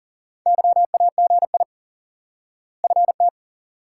Morse code, heard as a keyed tone, about 40 wpm, 700 Hz. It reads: YAGI FT
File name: IQE Course 3 - Lesson 89 - YAGI - Two Elements - 40wpm